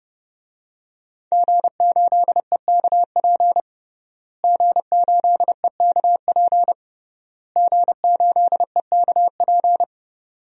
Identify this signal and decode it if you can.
Morse code, heard as a keyed tone, about 30 wpm, 700 Hz. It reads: G8EKP G8EKP G8EKP
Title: DXCC-100 Call Signs - 5 Characters - Repeated 3x - 30wpm